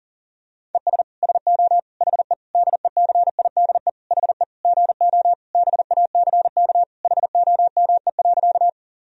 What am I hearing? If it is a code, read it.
Morse code, heard as a keyed tone, about 40 wpm, 700 Hz. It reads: ES SO HE DECIDE HE GO BACK HOME.
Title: Sentences from Top 500 Words and Abbreviations 40wpm